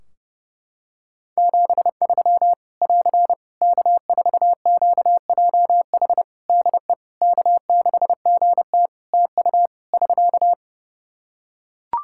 Morse code, heard as a keyed tone, about 30 wpm, 700 Hz. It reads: 73 <AR> K4QJH DE K6GT TU <SK>
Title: Practice QSO - Exchange 4 - Ending - 30wpm